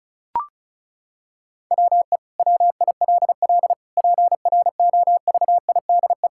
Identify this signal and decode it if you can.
Morse code, heard as a keyed tone, about 34 wpm, 700 Hz. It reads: WE WILL PROVIDE
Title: Sentences from Top 1000 Words - Review Part 5 - 35wpm